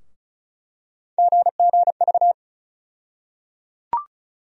Morse code, heard as a keyed tone, about 36 wpm, 700 Hz. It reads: GGV
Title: USA Call Sign Postfixes - 3 Letters 35wpm